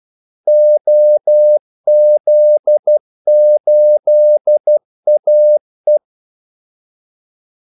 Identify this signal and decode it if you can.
Morse code, heard as a keyed tone, about 12 wpm, 600 Hz. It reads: OZ8AE
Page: 2 ekstra ordmellemrum mellem kaldesignalerne